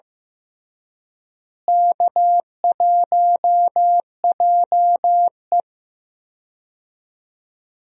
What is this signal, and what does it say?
Morse code, heard as a keyed tone, about 15 wpm, 700 Hz. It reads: K1JE
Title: USA Call Signs - 4 Characters - Repeated 3x - 15wpm